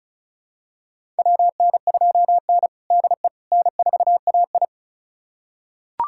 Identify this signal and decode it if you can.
Morse code, heard as a keyed tone, about 36 wpm, 700 Hz. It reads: WN2N DE N4AI
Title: USA Callsign DE Callsign - 4 Characters - 35wpm